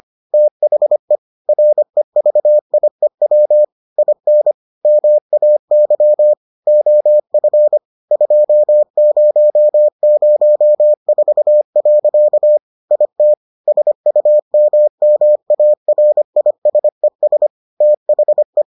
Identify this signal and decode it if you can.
Morse code, heard as a keyed tone, about 25 wpm, 600 Hz. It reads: THE REVIEW IN MAY OF 2004. IT SUMMARISES THE